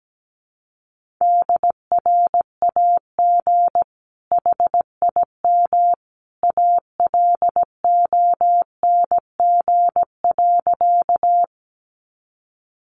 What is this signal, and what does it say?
Morse code, heard as a keyed tone, about 17 wpm, 700 Hz. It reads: DRAG HIM ALONG.
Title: Sentences from Top 1500 Words 17wpm